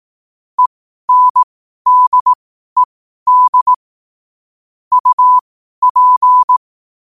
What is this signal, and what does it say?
Morse code, heard as a keyed tone, about 18 wpm, spaced as for 12 wpm, 1000 Hz. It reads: ENDED UP
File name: qotdfiles / 12wpm